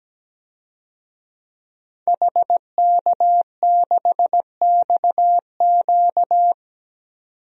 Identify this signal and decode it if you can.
Morse code, heard as a keyed tone, about 17 wpm, 700 Hz. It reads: HK6XQ
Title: DXCC-100 Call Signs - 5 Characters - 17wpm